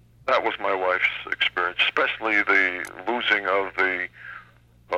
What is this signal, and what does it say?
Emergency Alert System (SAME) tones, heard as harsh buzzing bursts, with speech. It is heard from 0.9 to 4.5 s.